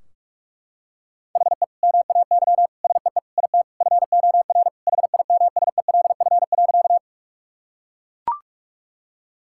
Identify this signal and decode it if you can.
Morse code, heard as a keyed tone, about 43 wpm, 700 Hz. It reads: HE MAY SEE IT FOR HIMSELF.
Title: Sentences from Top 1600 Words 45wpm